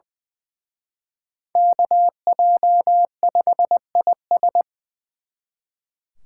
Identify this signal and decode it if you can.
Morse code, heard as a keyed tone, about 24 wpm, 700 Hz. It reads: KJ5IS
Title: Speed Racing - USA 2x2 Call Signs 20wpm